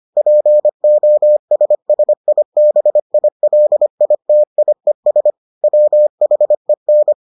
Morse code, heard as a keyed tone, about 25 wpm, 600 Hz. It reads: POSSIBILITIES WHEN